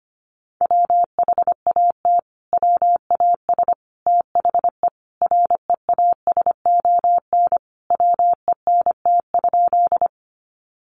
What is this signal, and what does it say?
Morse code, heard as a keyed tone, about 25 wpm, 700 Hz. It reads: WHAT WAS THE REASON WENT?
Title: Sentences from Top 700 Words 25wpm